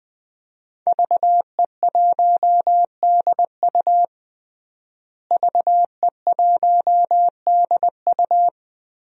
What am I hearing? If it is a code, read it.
Morse code, heard as a keyed tone, about 20 wpm, 700 Hz. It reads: VE1DU VE1DU
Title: DXCC-100 Call Signs - 5 Characters - Repeated 2x - 20wpm